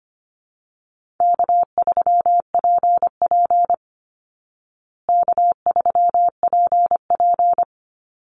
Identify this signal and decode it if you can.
Morse code, heard as a keyed tone, about 25 wpm, 700 Hz. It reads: K3PP K3PP
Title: Easy Call Signs Repeated 2x - 25wpm